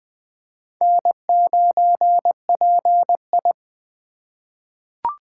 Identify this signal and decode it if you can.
Morse code, heard as a keyed tone, about 20 wpm, 700 Hz. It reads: N9PI
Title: USA Call Signs - 4x Character Spacing - 20wpm